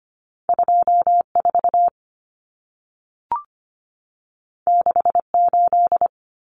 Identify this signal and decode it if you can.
Morse code, heard as a keyed tone, about 25 wpm, 700 Hz. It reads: 24 68
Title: Numbers - 2-Digit 25wpm